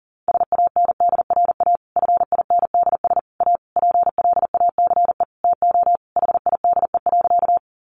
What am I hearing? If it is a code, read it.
Morse code, heard as a keyed tone, about 40 wpm, 700 Hz. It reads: SANDRA FINDS A PLACE TO HIDE.
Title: Easy Phrases 40wpm